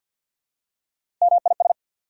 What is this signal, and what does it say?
Morse code, heard as a keyed tone, about 48 wpm, 700 Hz. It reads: MIS
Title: Prefixes and Suffixes 50wpm